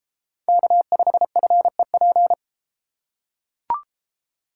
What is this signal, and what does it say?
Morse code, heard as a keyed tone, about 31 wpm, 700 Hz. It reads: K5FEP